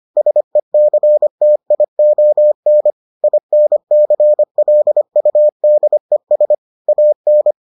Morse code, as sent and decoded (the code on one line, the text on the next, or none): ... -.-. - .. --- -. / .. -. -.-. .-.. ..- -.. ... / .- -.
SECTION INCLUDES AN